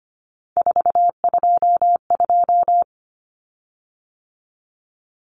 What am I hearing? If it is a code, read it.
Morse code, heard as a keyed tone, about 25 wpm, 700 Hz. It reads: 422
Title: Common Cut Numbers - 3-Digit 25wpm